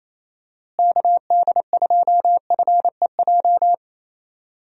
Morse code, heard as a keyed tone, about 28 wpm, 700 Hz. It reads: KD2FEJ
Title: USA Call Signs - 6 Characters - Repeated 2x - 28wpm